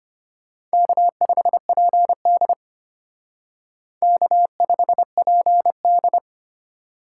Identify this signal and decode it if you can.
Morse code, heard as a keyed tone, about 32 wpm, 700 Hz. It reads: K5PD K5PD